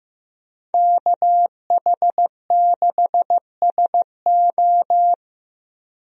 Morse code, heard as a keyed tone, about 15 wpm, 700 Hz. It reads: KH6SO
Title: USA Call Signs - 5 Characters - Repeated 3x - 15wpm